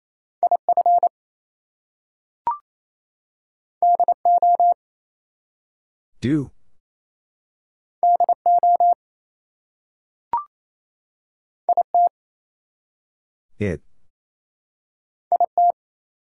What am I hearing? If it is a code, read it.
Morse code, heard as a keyed tone, about 28 wpm, 700 Hz. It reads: IF DO DO IT IT
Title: Two Letter Words - 28wpm